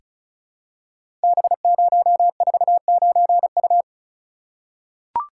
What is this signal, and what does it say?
Morse code, heard as a keyed tone, about 36 wpm, 700 Hz. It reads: B049U
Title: Code Groups - 5 Letter:Numbers - 35wpm